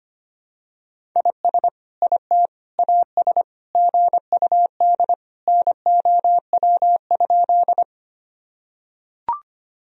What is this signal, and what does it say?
Morse code, heard as a keyed tone, about 25 wpm, 700 Hz. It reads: IS IT AS GUD NOW?
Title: Sentences from Top 200 Words and Abbreviations 25wpm